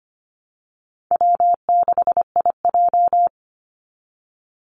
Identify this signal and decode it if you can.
Morse code, heard as a keyed tone, about 24 wpm, 700 Hz. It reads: W6IJ
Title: Easy Call Signs - 25wpm